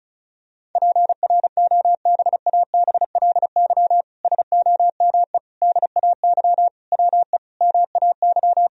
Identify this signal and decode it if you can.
Morse code, heard as a keyed tone, about 36 wpm, 700 Hz. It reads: PROBABLY SOME DAY WE MAY
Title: Sentences from Top 1000 Words - Review Part 4 - 35wpm